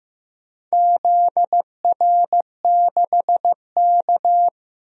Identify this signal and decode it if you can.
Morse code, heard as a keyed tone, about 15 wpm, 700 Hz. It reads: ZR6K
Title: DXCC-100 Call Signs - 4 Characters - Repeated 3x - 15wpm